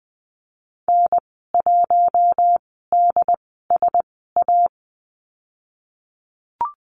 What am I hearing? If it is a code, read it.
Morse code, heard as a keyed tone, about 20 wpm, spaced as for 14 wpm, 700 Hz. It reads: N1DSA
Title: 088 Lesson 15g - Number 5 - Call Signs 20wpm